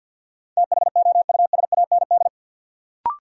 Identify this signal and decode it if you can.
Morse code, heard as a keyed tone, about 52 wpm, 700 Hz. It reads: THOUSAND
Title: Sentences from Top 600 Words - New Word Drill - 50wpm